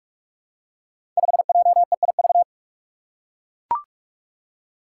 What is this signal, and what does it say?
Morse code, heard as a keyed tone, about 44 wpm, 700 Hz. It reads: 5JEIV
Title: Five Challenging Letter-Number Combinations 45wpm